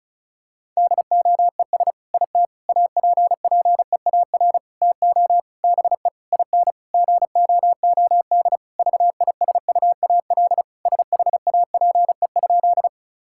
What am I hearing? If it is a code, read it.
Morse code, heard as a keyed tone, about 36 wpm, 700 Hz. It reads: DOES IT APPEAR TO BE IN GOOD VISUAL SHAPE?
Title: Sentences from Top 1000 Words - Review Part 6 - 35wpm